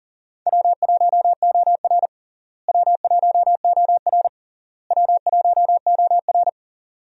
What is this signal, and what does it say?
Morse code, heard as a keyed tone, about 40 wpm, 700 Hz. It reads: W1OR W1OR W1OR
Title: USA Call Signs - 4 Characters - Repeated 3x - 40wpm